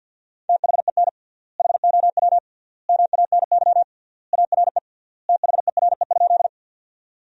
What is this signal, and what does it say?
Morse code, heard as a keyed tone, about 47 wpm, 700 Hz. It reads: THEN HOW MANY ARE THERE?
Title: 3x Word-Spacing - Sentences from Top 100 Words - 50wpm